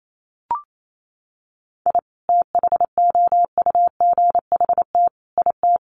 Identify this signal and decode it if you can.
Morse code, heard as a keyed tone, about 28 wpm, 700 Hz. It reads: I THOUGHT IT
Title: Sets of 5 Words 28wpm